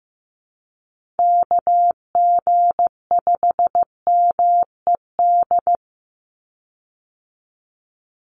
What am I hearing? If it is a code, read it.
Morse code, heard as a keyed tone, about 15 wpm, 700 Hz. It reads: KG5MED
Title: USA Call Signs 15wpm